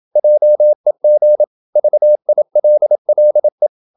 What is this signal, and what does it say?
Morse code, heard as a keyed tone, about 27 wpm, 600 Hz. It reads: JEG VILLE
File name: Ferie QRQ 27wpm